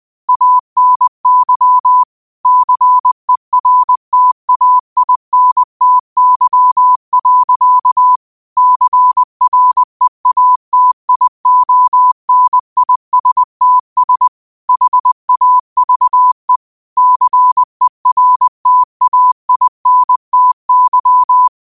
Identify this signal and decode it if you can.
Morse code, heard as a keyed tone, about 20 wpm, 1000 Hz. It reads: ANY CERTAINTY. CREATIONISTS HAVE CERTAINTY
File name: qotdfiles / 20wpm